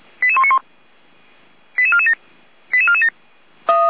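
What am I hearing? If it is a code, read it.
Five-tone selective call (ZVEI), heard as quick runs of five tones, tones of about 70 ms. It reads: 88191 88447 88447
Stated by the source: ZVEI five-tone selective call received by SDR.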